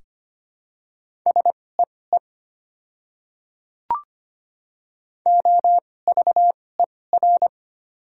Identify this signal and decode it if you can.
Morse code, heard as a keyed tone, about 25 wpm, spaced as for 18 wpm, 700 Hz. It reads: SEE OVER
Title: 2x Character Spacing - Top 200 Words - 25wpm